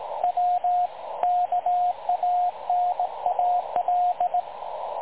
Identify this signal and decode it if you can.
Morse code, heard as a keyed tone, about 19 wpm, 720 Hz, with radio noise. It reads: W KANAL
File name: dlazaawansowanych / nagraniazpasma / websdr recording 3565.0kHz